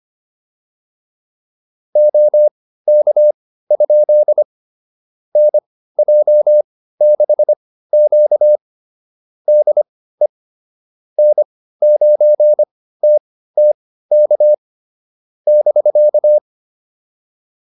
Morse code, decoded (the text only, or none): OK? NJ6Q DE N9TTK <BK>